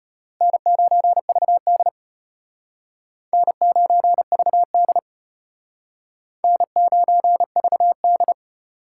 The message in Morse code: N9VD N9VD N9VD

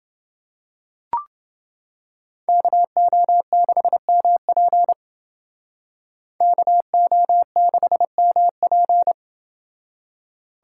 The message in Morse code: KO6MP KO6MP

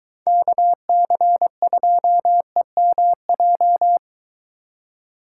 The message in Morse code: KC2EMJ

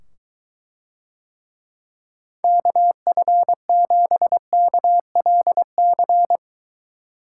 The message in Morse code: KF7KLC

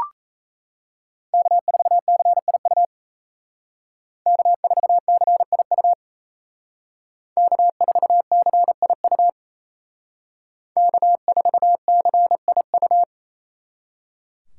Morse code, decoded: K4CIU K4CIU K4CIU K4CIU

